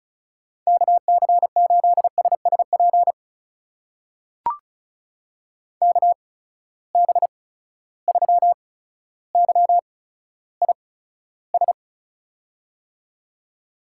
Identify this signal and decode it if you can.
Morse code, heard as a keyed tone, about 36 wpm, spaced as for 15 wpm, 700 Hz. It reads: KC8SSP KB3YIS
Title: USA Call Signs - 8x Character Spacing - 35wpm